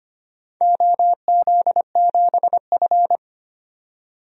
Morse code, heard as a keyed tone, about 25 wpm, 700 Hz. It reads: OZ7F
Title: DXCC-100 Call Signs - 4 Characters - 25wpm